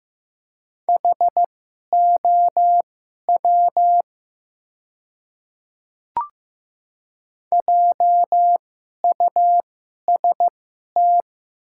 Morse code, heard as a keyed tone, about 15 wpm, spaced as for 11 wpm, 700 Hz. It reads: HOW JUST
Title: 2x Character Spacing - Top 300 Words - 15wpm